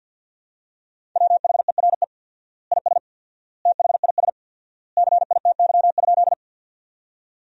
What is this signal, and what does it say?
Morse code, heard as a keyed tone, about 50 wpm, 700 Hz. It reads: WHERE IS THIS CITY?
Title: 4x Word-Spacing - Sentences from Top 300 Words - 50wpm